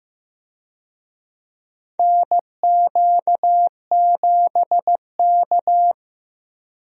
Morse code, heard as a keyed tone, about 16 wpm, 700 Hz. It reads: NQ7K
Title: USA Call Signs - 4 Characters - 15wpm